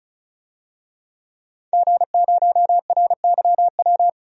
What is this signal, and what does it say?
Morse code, heard as a keyed tone, about 36 wpm, 700 Hz. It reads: G0RYW
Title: DXCC-100 Call Signs - 5 Characters - 35wpm